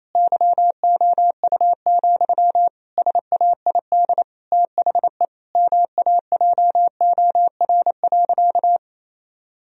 Morse code, heard as a keyed tone, about 28 wpm, 700 Hz. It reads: YOU, SAID THE MAJOR.
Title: Sentences from Top 1000 Words - Review Part 4 - 28wpm